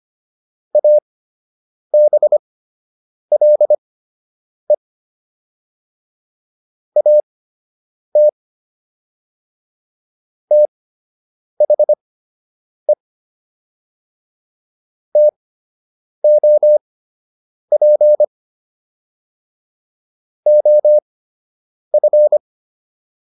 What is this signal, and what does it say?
Morse code, heard as a keyed tone, about 25 wpm, spaced as for 8 wpm, 600 Hz. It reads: ABLE AT THE TOP OF